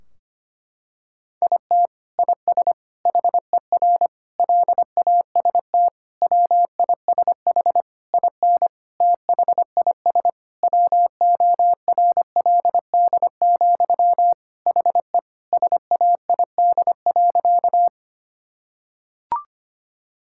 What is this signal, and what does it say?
Morse code, heard as a keyed tone, about 25 wpm, 700 Hz. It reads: IT IS HER LAST WISH IN THIS WORLD, HE SAID.
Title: Sentences from Top 1600 Words 25wpm